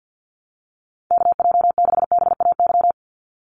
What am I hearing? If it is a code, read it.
Morse code, heard as a keyed tone, about 49 wpm, 700 Hz. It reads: KJ6BAY
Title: USA Call Signs 50wpm